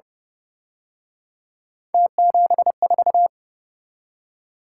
Morse code, heard as a keyed tone, about 29 wpm, 700 Hz. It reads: T74
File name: Speed-Racing - Three Letter-Number Combinations - 25wpm